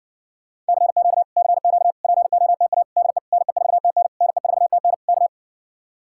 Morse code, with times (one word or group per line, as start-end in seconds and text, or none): 0.7-1.2 CQ
1.4-1.9 CQ
2.0-2.8 POTA
3.0-3.2 DE
3.3-4.1 NE4TN
4.2-4.9 NE4TN
5.1-5.3 K